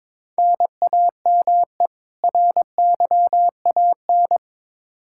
Morse code, heard as a keyed tone, about 22 wpm, 700 Hz. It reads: NAME RYAN
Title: K1USN SST QSO Practice 22wpm